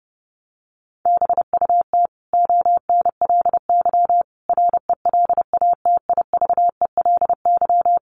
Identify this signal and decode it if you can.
Morse code, heard as a keyed tone, about 30 wpm, 700 Hz. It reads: BUT ONLY RELATIVELY
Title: Sentences from Top 1100 Words 30wpm